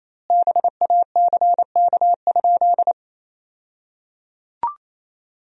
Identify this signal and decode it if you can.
Morse code, heard as a keyed tone, about 27 wpm, 700 Hz. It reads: BACK?
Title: 2x Word-Spacing - Sentences from Top 300 Words - 28wpm